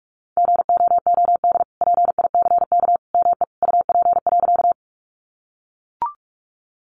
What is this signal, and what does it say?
Morse code, heard as a keyed tone, about 44 wpm, 700 Hz. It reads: GOOD PICK ME UP.
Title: Sentences from Top 1000 Words 45wpm